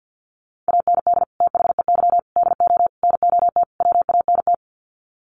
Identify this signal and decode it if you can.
Morse code, heard as a keyed tone, about 50 wpm, 700 Hz. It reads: AND THEY DO NOT WANT
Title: Sets of 5 Words Encore2 50wpm